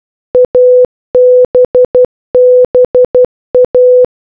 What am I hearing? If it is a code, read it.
Morse code, heard as a keyed tone, about 12 wpm, 500 Hz. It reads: ABBA